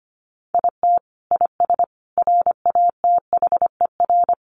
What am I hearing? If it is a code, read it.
Morse code, heard as a keyed tone, about 25 wpm, 700 Hz. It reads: IT IS RATHER
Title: Sentences from Top 700 Words 25wpm